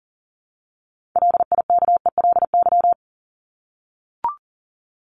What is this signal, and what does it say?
Morse code, heard as a keyed tone, about 40 wpm, 700 Hz. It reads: LIKELY
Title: LY Suffix - Top 100 Words with Base Words 40wpm